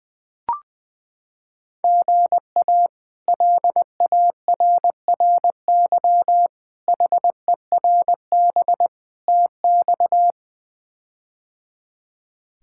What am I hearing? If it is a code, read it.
Morse code, heard as a keyed tone, about 21 wpm, 700 Hz. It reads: GA LARRY HERB TX
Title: K1USN SST QSO Practice - 20wpm